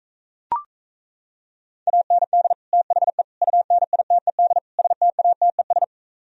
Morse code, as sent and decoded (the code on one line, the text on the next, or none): .- -. -.. / - .... / ..- -. .. - . -.. / ... - .- - . ...
AND THE UNITED STATES